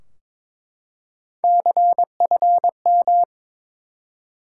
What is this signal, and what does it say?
Morse code, heard as a keyed tone, about 21 wpm, 700 Hz. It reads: CFM
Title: Top 90 QSO Elements 22wpm